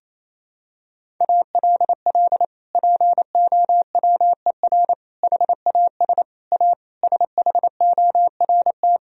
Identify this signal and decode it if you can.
Morse code, heard as a keyed tone, about 28 wpm, 700 Hz. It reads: ALL POWER HAS A SHORT
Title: Sentences from Top 1000 Words - Review Part 2 - 28wpm